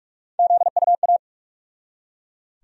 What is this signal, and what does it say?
Morse code, heard as a keyed tone, about 41 wpm, 700 Hz. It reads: ZUA
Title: USA Call Sign Postfixes - 3 Letters 45wpm